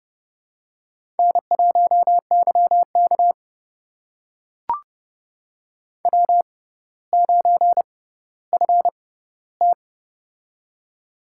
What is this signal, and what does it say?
Morse code, heard as a keyed tone, about 28 wpm, spaced as for 15 wpm, 700 Hz. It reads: N1YK W9FT
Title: USA Call Signs - 4 Characters - 6x Character Spacing - 30wpm